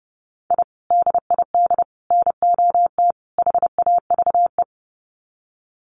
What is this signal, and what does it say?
Morse code, heard as a keyed tone, about 30 wpm, 700 Hz. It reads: I DID NOT HAVE
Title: Sets of 4 Words 30wpm